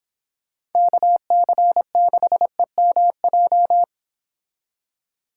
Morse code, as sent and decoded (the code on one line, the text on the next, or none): -.- -.-. -.... -- .---
KC6EMJ